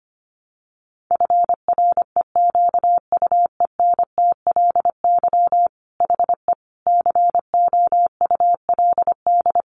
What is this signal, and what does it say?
Morse code, heard as a keyed tone, about 25 wpm, 700 Hz. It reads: FREQUENTLY HE COULD
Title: Sentences from Top 1300 Words 25wpm